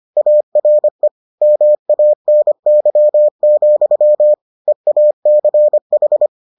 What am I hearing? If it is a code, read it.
Morse code, heard as a keyed tone, about 25 wpm, 600 Hz. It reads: ARE MANY, EACH